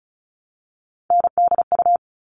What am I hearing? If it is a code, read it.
Morse code, heard as a keyed tone, about 34 wpm, 700 Hz. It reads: NDU
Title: All Cut Numbers - 3-Digit 35wpm